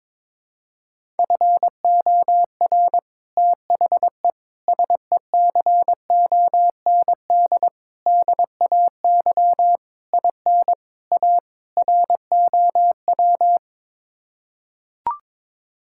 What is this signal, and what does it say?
Morse code, heard as a keyed tone, about 22 wpm, 700 Hz. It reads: FOR THE SECOND DAY IN A ROW
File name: practice / 22wpm / Sets of 7 Words - 22wpm